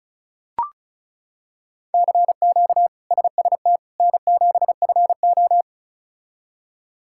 Morse code, heard as a keyed tone, about 36 wpm, 700 Hz. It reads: CQ SST N7FO
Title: K1USN SST QSO Practice - 35wpm